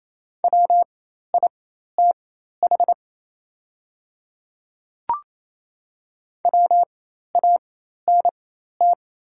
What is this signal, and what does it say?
Morse code, heard as a keyed tone, about 27 wpm, spaced as for 13 wpm, 700 Hz. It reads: WITH WANT
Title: 4x Character Spacing - Top 200 Words - 28wpm